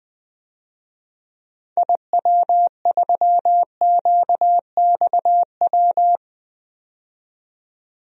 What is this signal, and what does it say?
Morse code, heard as a keyed tone, about 20 wpm, 700 Hz. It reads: IW3QXW
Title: DXCC-100 Call Signs - 6 Characters - 20wpm